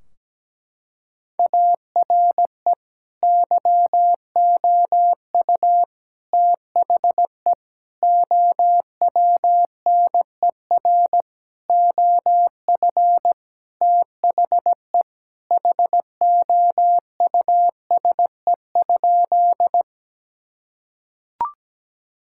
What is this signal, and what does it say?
Morse code, heard as a keyed tone, about 17 wpm, 700 Hz. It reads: ARE YOU THE OWNER OF THE HOUSE?
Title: Sentences from Top 1600 Words 17wpm